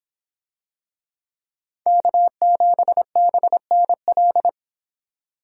Morse code, heard as a keyed tone, about 24 wpm, 700 Hz. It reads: K7BNL